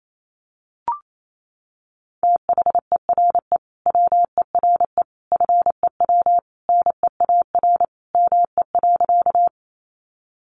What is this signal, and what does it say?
Morse code, heard as a keyed tone, about 28 wpm, 700 Hz. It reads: THERE WERE FEW NEAR ME.
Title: Sentences from Top 300 Words 28wpm